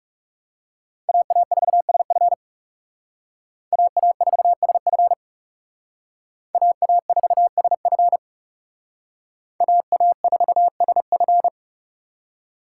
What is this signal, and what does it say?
Morse code, heard as a keyed tone, about 37 wpm, 700 Hz. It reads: AA4SF AA4SF AA4SF AA4SF
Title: Speed Racing - USA 2x2 Call Signs 30wpm